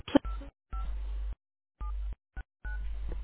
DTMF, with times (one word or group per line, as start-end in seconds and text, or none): 0.7-0.8 5
1.8-2.8 79